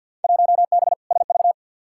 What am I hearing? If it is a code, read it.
Morse code, heard as a keyed tone, about 51 wpm, 700 Hz. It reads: 1B SV
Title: Field Day Exchanges Encore - 50wpm